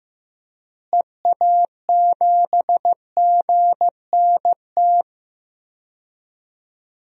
Morse code, heard as a keyed tone, about 15 wpm, 700 Hz. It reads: EA7GNT
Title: DXCC-100 Call Signs - 6 Characters - Repeated 3x - 15wpm